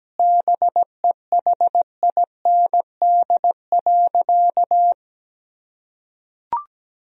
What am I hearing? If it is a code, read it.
Morse code, heard as a keyed tone, about 17 wpm, 700 Hz. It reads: BEHIND.
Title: Sentences from Top 1000 Words - Review Part 1 - 17wpm